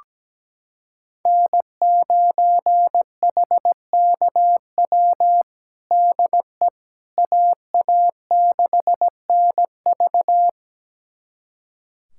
Morse code, heard as a keyed tone, about 17 wpm, 700 Hz. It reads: N9HKW DE AA6NV
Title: USA Callsign DE Callsign - 5 Characters - 17wpm